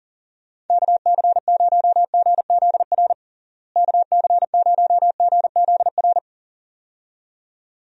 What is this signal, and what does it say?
Morse code, heard as a keyed tone, about 40 wpm, 700 Hz. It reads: KC0GZR KC0GZR
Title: USA Call Signs - 6 Characters - Repeated 2x - 40wpm